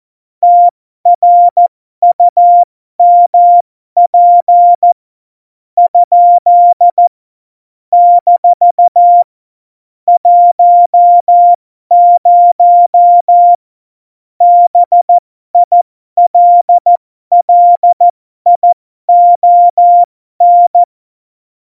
Morse code, heard as a keyed tone, about 14 wpm, spaced as for 12 wpm, 700 Hz. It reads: TRUMP ? - 10 BILLION
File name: Morsekurs / Witze